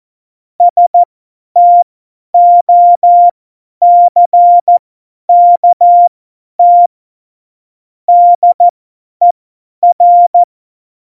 Morse code, heard as a keyed tone, about 14 wpm, spaced as for 10 wpm, 700 Hz. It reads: STOCKT DER